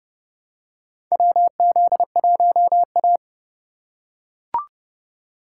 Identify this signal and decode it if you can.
Morse code, heard as a keyed tone, about 30 wpm, 700 Hz. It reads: WZ1A
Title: USA Call Signs - 4 Characters - 4x Character Spacing - 30wpm